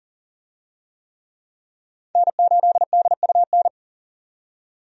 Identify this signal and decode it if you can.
Morse code, heard as a keyed tone, about 40 wpm, 700 Hz. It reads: N8DUN
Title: USA Call Signs - 5 Characters - 40wpm